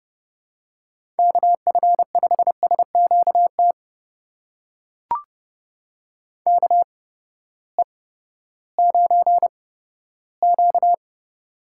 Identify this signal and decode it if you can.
Morse code, heard as a keyed tone, about 30 wpm, spaced as for 15 wpm, 700 Hz. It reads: KF5SQT KE9Q
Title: USA Call Signs - 8x Character Spacing - 30wpm